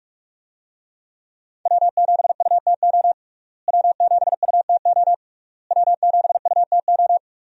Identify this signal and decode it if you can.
Morse code, heard as a keyed tone, about 46 wpm, 700 Hz. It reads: W7UTO W7UTO W7UTO
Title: USA Call Signs - 5 Characters - Repeated 3x - 45wpm